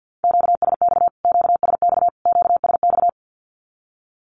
Morse code, heard as a keyed tone, about 48 wpm, 700 Hz. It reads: QSX QSX QSX
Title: Q-Codes Repeated 3x 50wpm